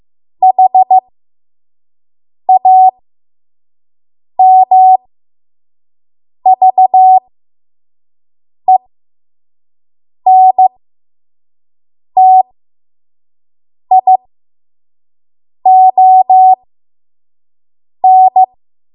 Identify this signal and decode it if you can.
Morse code, heard as a keyed tone, about 15 wpm, spaced as for 5 wpm, 750 Hz. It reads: HAMVENTION
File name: Morse / 05WPM